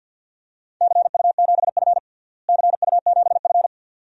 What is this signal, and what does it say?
Morse code, heard as a keyed tone, about 48 wpm, 700 Hz. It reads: CU7F CU7F
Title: DXCC-100 Call Signs - 4 Characters - Repeated 2x - 50wpm